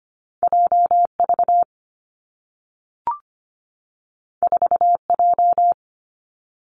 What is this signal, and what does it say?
Morse code, heard as a keyed tone, about 25 wpm, 700 Hz. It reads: JV 4J
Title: Two Challenging Letter-Number Combinations 25wpm